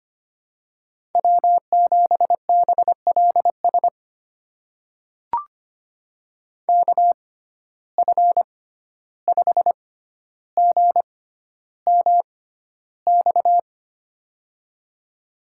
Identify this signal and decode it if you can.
Morse code, heard as a keyed tone, about 25 wpm, spaced as for 12 wpm, 700 Hz. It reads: W7BLS KF5GMX
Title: USA Call Signs - 6x Character Spacing - 25wpm